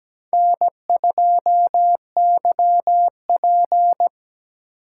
Morse code, heard as a keyed tone, about 20 wpm, 700 Hz. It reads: N2YP
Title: Speed Racing - USA 1x2 Call Signs 17wpm